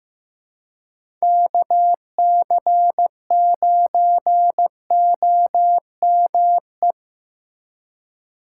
Morse code, heard as a keyed tone, about 15 wpm, 700 Hz. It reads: KC9OME
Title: USA Call Signs - 6 Characters - 2x Character Spacing - 15wpm